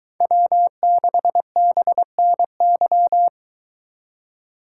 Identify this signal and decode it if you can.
Morse code, heard as a keyed tone, about 22 wpm, 700 Hz. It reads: W6BNY